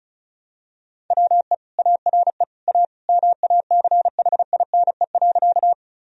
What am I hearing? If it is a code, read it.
Morse code, heard as a keyed tone, about 36 wpm, 700 Hz. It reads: WE ARE A MACHINE.
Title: Sentences from Top 1000 Words - Review Part 2 - 35wpm